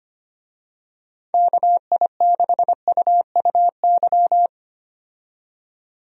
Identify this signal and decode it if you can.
Morse code, heard as a keyed tone, about 25 wpm, 700 Hz. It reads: KI6UUY
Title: USA Call Signs - 6 Characters - 25wpm